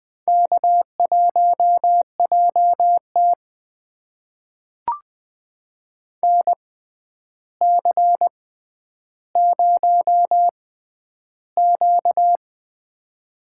Morse code, heard as a keyed tone, about 20 wpm, spaced as for 10 wpm, 700 Hz. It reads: K1JT NC0Q
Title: USA Call Signs - 4 Characters - 6x Character Spacing - 20wpm